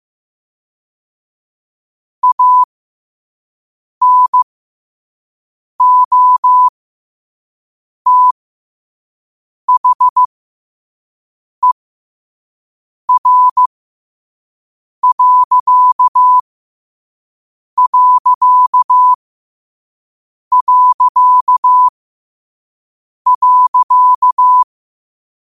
Morse code, heard as a keyed tone, about 15 wpm, spaced as for 5 wpm, 1000 Hz. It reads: ANOTHER....
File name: qotdfiles / 5wpm